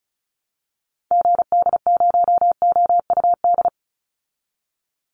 Morse code, heard as a keyed tone, about 34 wpm, 700 Hz. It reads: GD0OUD
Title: Call Signs Repeated 2x - 35wpm